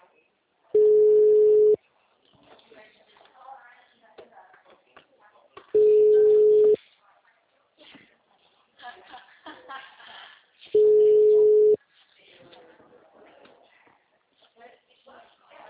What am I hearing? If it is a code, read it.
Ringback tone (Europe, 425 Hz), with speech.